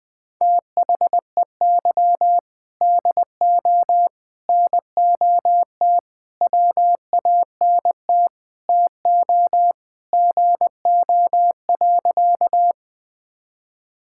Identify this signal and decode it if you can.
Morse code, heard as a keyed tone, about 20 wpm, 700 Hz. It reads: THEY DO NOT WANT TO GO.
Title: Sentences with 3-Word Phrases - Spoken Beginning - 20wpm